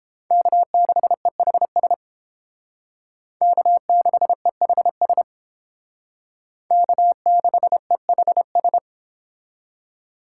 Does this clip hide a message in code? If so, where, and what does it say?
Yes: Morse code from 0.3 to 8.8 s, heard as a keyed tone, about 30 wpm, 700 Hz. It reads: K6EHS K6EHS K6EHS